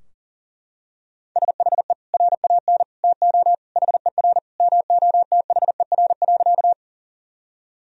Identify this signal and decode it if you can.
Morse code, heard as a keyed tone, about 40 wpm, 700 Hz. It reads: SHE RAN TO HER MOTHER.
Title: Sentences from Top 1000 Words - Review Part 3 - 40wpm